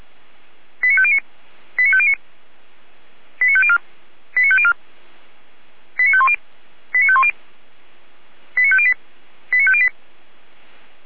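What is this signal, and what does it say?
Five-tone selective call (ZVEI), heard as quick runs of five tones, tones of about 70 ms. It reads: 89509 89509 89504 89504 89510 89510 89508 89508
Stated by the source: Received on an SDR as ZVEI five-tone call, address 89501.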